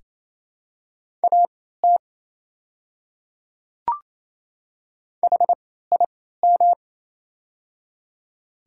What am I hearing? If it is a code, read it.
Morse code, heard as a keyed tone, about 27 wpm, spaced as for 16 wpm, 690 Hz. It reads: AT HIM
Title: 3x Character Spacing - Top 100 Words - 28wpm